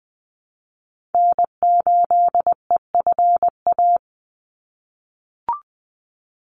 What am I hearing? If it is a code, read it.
Morse code, heard as a keyed tone, about 20 wpm, 700 Hz. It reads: N8EFA
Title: USA Call Signs 20wpm